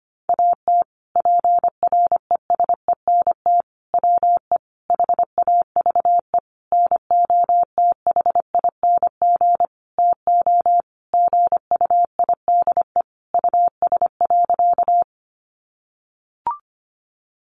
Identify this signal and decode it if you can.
Morse code, heard as a keyed tone, about 25 wpm, 700 Hz. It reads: AT PRESENT WE HAVE NOTHING TO GUIDE US.
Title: Sentences from Top 800 Words 25wpm